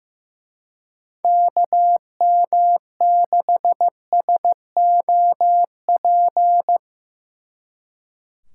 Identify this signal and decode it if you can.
Morse code, heard as a keyed tone, about 17 wpm, 700 Hz. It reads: KM6SOP
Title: Speed Racing - USA 2x3 Call Signs 15wpm